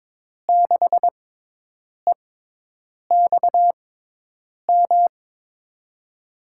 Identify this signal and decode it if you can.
Morse code, heard as a keyed tone, about 23 wpm, spaced as for 8 wpm, 700 Hz. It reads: 6EXM
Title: USA Call Signs - 6 Characters - 6x Character Spacing - 22wpm